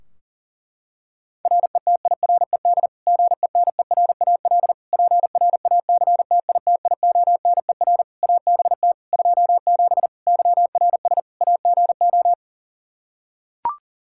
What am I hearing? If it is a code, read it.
Morse code, heard as a keyed tone, about 37 wpm, 700 Hz. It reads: RETIRED GENERAL PRACTITIONER ABT 27 YRS AGO
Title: Practice QSO - Exchange 3 - More Info - 40wpm